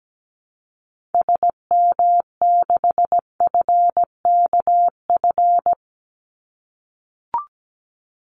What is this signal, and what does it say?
Morse code, heard as a keyed tone, about 17 wpm, 700 Hz. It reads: SM6FKF
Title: Call Signs - 17wpm